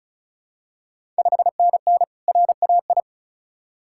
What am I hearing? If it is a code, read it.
Morse code, heard as a keyed tone, about 36 wpm, 700 Hz. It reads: 5NN RAI
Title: Ontario QSO Party Exchange 35wpm